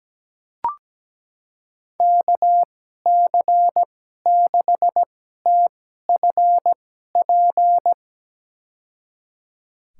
Morse code, heard as a keyed tone, about 17 wpm, spaced as for 12 wpm, 700 Hz. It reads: KC6TFP
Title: USA Call Signs - 6 Characters - 2x Character Spacing - 17wpm